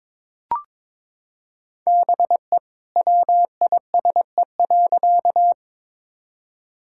Morse code, heard as a keyed tone, about 21 wpm, 700 Hz. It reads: BE WISE.
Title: Sentences from Top 1600 Words 22wpm